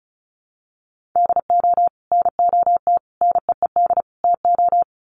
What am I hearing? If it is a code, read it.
Morse code, heard as a keyed tone, about 34 wpm, 700 Hz. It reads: DO NOT NEED TO
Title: Sets of 4 Words 35wpm